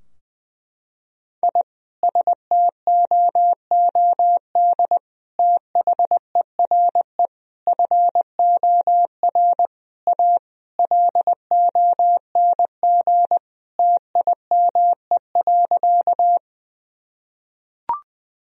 Morse code, heard as a keyed tone, about 20 wpm, 700 Hz. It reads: I STOOD THERE FOR A LONG TIME.
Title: Sentences from Top 1000 Words - Review Part 2 - 20wpm